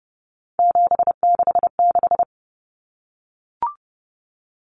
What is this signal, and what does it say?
Morse code, heard as a keyed tone, about 30 wpm, 700 Hz. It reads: 766
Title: Numbers - 3-Digit 30wpm